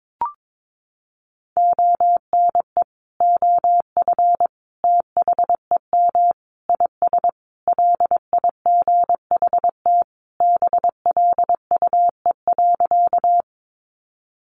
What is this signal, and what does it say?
Morse code, heard as a keyed tone, about 22 wpm, 700 Hz. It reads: ONE OF THEM IS LIGHT BLUE.
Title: Additional Easy Phrases 22wpm